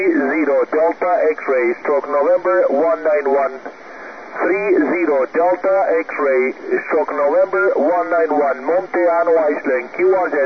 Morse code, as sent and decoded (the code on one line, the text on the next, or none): none